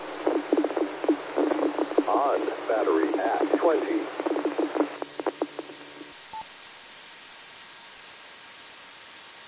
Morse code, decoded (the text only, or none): none